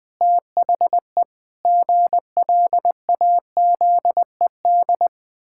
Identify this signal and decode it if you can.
Morse code, heard as a keyed tone, about 20 wpm, 700 Hz. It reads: THE GLAZED
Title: Pangram Challenge v2 - 20wpm